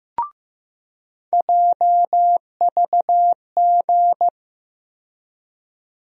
Morse code, heard as a keyed tone, about 15 wpm, 700 Hz. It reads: JVG
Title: USA Call Sign Postfixes - 3 Letters 15wpm